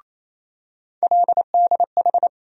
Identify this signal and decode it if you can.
Morse code, heard as a keyed tone, about 28 wpm, 700 Hz. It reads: LDH
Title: USA Call Sign Postfixes - 3 Letters 28wpm